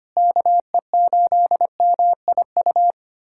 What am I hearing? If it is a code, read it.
Morse code, heard as a keyed tone, about 25 wpm, 700 Hz. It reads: KE8MIU
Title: USA Callsign DE Callsign - 6 Characters - 25wpm